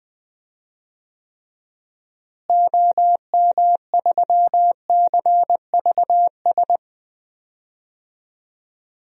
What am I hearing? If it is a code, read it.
Morse code, heard as a keyed tone, about 20 wpm, 700 Hz. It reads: OM3CVS
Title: DXCC-100 Call Signs - 6 Characters - Repeated 3x - 20wpm